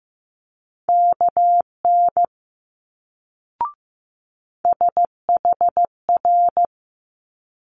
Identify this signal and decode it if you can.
Morse code, heard as a keyed tone, about 15 wpm, 700 Hz. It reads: KN SHR
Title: Multi-letter Phonemes Quick Fire 15wpm